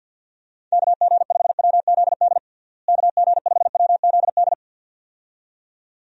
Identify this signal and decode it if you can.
Morse code, heard as a keyed tone, about 47 wpm, 700 Hz. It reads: KG5WZD KG5WZD
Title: USA Call Signs - 6 Characters - Repeated 2x - 50wpm